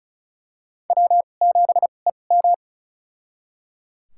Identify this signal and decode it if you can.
Morse code, heard as a keyed tone, about 34 wpm, spaced as for 25 wpm, 690 Hz. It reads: W7EM
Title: USA Call Signs - 4 Characters - 2x Character Spacing - 35wpm